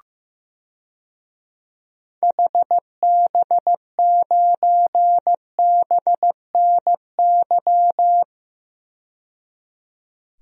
Morse code, heard as a keyed tone, about 16 wpm, 700 Hz. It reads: HB9BNY
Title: DXCC-100 Call Signs - 6 Characters - 15wpm